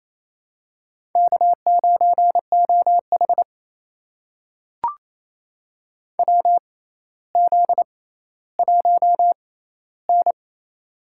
Morse code, heard as a keyed tone, about 28 wpm, spaced as for 14 wpm, 700 Hz. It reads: K9OH WZ1N
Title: USA Call Signs - 4 Characters - 6x Character Spacing - 28wpm